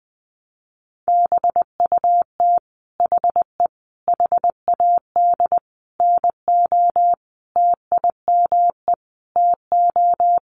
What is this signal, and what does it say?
Morse code, heard as a keyed tone, about 20 wpm, 700 Hz. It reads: BUT HE HAD NO TIME TO
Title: Sentences from Top 1200 Words 20wpm